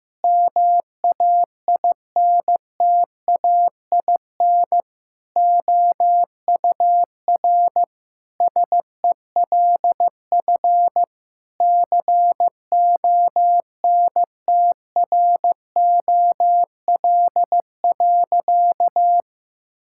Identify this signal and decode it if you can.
Morse code, heard as a keyed tone, about 15 wpm, 700 Hz. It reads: MAINTAIN OUR SELF CONTROL.
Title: Sentences from Top 1000 Words - Review Part 5 - 15wpm